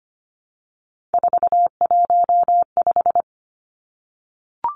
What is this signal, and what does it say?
Morse code, heard as a keyed tone, about 25 wpm, 700 Hz. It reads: 415
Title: Numbers - 3-Digit 25wpm